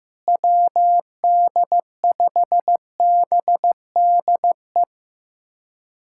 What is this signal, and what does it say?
Morse code, heard as a keyed tone, about 15 wpm, 700 Hz. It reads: WD5BDE
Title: USA Callsign DE Callsign 15wpm